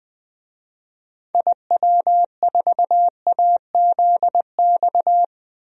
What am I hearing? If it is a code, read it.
Morse code, heard as a keyed tone, about 20 wpm, 700 Hz. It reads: IW4AZX
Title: DXCC-100 Call Signs - 6 Characters - 20wpm